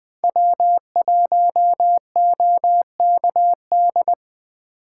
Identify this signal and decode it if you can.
Morse code, heard as a keyed tone, about 21 wpm, 700 Hz. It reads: W1OKD
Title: USA Call Signs - 5 Characters - 3x Character Spacing - 20wpm